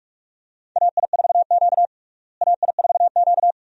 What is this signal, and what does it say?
Morse code, heard as a keyed tone, about 46 wpm, 700 Hz. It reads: AI4Q AI4Q
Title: USA Call Signs - 4 Characters - Repeated 2x - 45wpm